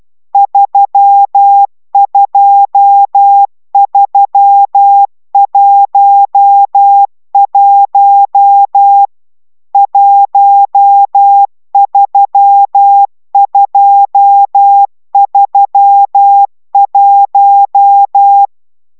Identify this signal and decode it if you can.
Morse code, heard as a keyed tone, about 12 wpm, 800 Hz. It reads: 32311 13231
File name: taviro tanulas / cw 69 22 10